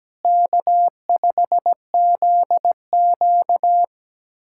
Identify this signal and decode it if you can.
Morse code, heard as a keyed tone, about 17 wpm, 700 Hz. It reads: K5ZQ
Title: USA Callsign DE Callsign - 4 Characters - 17wpm